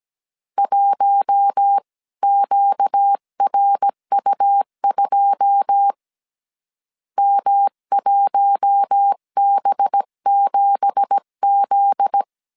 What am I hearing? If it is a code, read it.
Morse code, heard as a keyed tone, about 17 wpm, 800 Hz. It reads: 1QRU2 M1B7Z
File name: CW / DET / 1 9 1500C S B 16wpm 2856